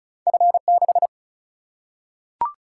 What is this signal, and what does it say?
Morse code, heard as a keyed tone, about 36 wpm, 700 Hz. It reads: F 6
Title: Single-Letter Single-Number 35wpm